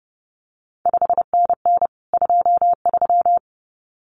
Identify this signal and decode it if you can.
Morse code, heard as a keyed tone, about 30 wpm, 700 Hz. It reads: 5NN 23
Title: CQ WW DX Contest Exchange Encore - 30wpm